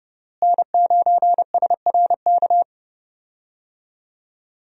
Morse code, heard as a keyed tone, about 30 wpm, 700 Hz. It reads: N9SRK
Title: USA Call Signs - 5 Characters - Repeated 2x - 30wpm